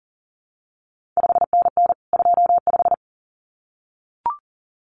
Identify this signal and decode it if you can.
Morse code, heard as a keyed tone, about 39 wpm, 700 Hz. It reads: 5NN 25
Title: CQ WW DX Contest Exchange Encore - 40wpm